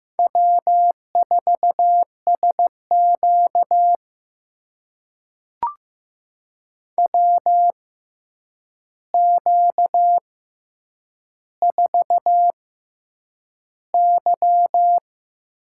Morse code, heard as a keyed tone, about 15 wpm, spaced as for 8 wpm, 700 Hz. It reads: W4SQ WQ4Y
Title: USA Call Signs - 4 Characters - 6x Character Spacing - 15wpm